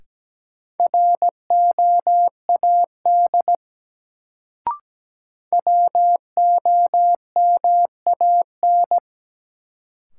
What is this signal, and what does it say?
Morse code, heard as a keyed tone, about 17 wpm, 700 Hz. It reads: ROAD WOMAN
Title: Sets of 5 Words - New Word Drill - 17wpm